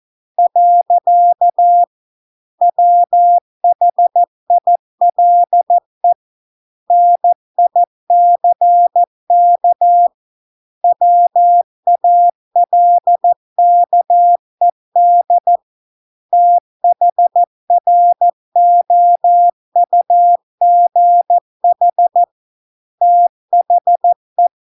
Morse code, heard as a keyed tone, about 14 wpm, 700 Hz. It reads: . WHILE NICK WALKED THROUGH THE